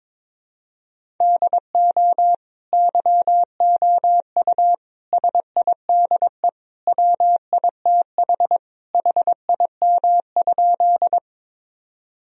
Morse code, heard as a keyed tone, about 22 wpm, 700 Hz. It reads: DO YOU SIDE WITH HIM?
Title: Sentences from Top 1000 Words - Review Part 1 - 22wpm